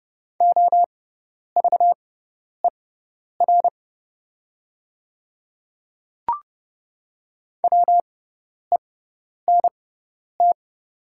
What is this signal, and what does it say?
Morse code, heard as a keyed tone, about 30 wpm, spaced as for 10 wpm, 700 Hz. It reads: OVER WENT
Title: 6x Character Spacing - Top 200 Words - 30wpm